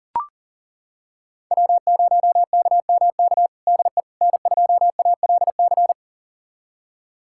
Morse code, heard as a keyed tone, about 40 wpm, 700 Hz. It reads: W0KMK DE N2ALC
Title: USA Callsign DE Callsign - 5 Characters - 40wpm